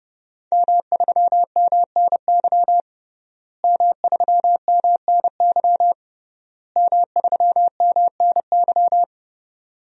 Morse code, heard as a keyed tone, about 32 wpm, 700 Hz. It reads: M3MNY M3MNY M3MNY
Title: DXCC-100 Call Signs - 5 Characters - Repeated 3x - 30wpm